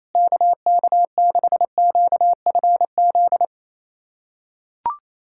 Morse code, heard as a keyed tone, about 28 wpm, 700 Hz. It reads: KK6QFZ
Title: USA Callsign DE Callsign 28wpm